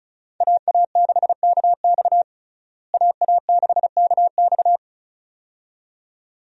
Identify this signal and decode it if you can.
Morse code, heard as a keyed tone, about 34 wpm, 700 Hz. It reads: AA6KX AA6KX
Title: USA Call Signs - 5 Characters - Repeated 2x - 35wpm